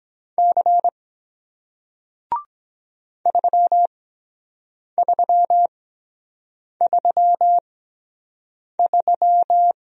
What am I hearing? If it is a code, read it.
Morse code, heard as a keyed tone, about 23 wpm, spaced as for 7 wpm, 700 Hz. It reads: C 3333